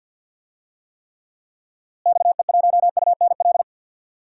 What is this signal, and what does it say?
Morse code, heard as a keyed tone, about 51 wpm, 700 Hz. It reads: XE1UNL
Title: DXCC-100 Call Signs - 6 Characters - Repeated 3x - 50wpm